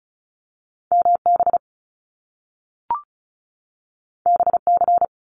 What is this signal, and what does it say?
Morse code, heard as a keyed tone, about 36 wpm, 700 Hz. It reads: MB BC
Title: Canadian Provinces and Territories 35wpm